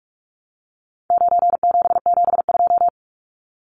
Morse code, heard as a keyed tone, about 44 wpm, 700 Hz. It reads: 9772
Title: Numbers - 4-Digit 45wpm